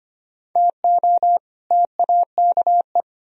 Morse code, heard as a keyed tone, about 33 wpm, 700 Hz. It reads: TO TAKE